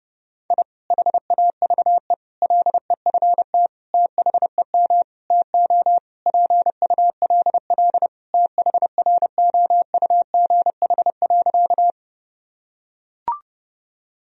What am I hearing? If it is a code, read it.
Morse code, heard as a keyed tone, about 30 wpm, 700 Hz. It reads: I HAVE LEFT THEM TO PULL THROUGH.
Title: Sentences from Top 1000 Words - Review Part 2 - 30wpm